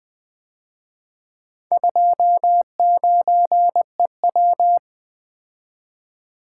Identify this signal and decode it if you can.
Morse code, heard as a keyed tone, about 24 wpm, 700 Hz. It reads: 29EW
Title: Speed-Racing - Four Letter-Number Combinations - 20wpm